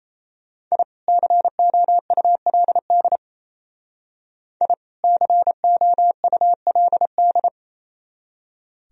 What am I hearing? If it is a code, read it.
Morse code, heard as a keyed tone, about 33 wpm, 700 Hz. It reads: I COULD I COULD
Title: Speed-Racing - Sets of 2 Words 28wpm